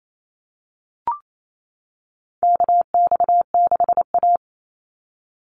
Morse code, heard as a keyed tone, about 28 wpm, 700 Hz. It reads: KX6A
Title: NA SOTA Chaser Callsigns 28wpm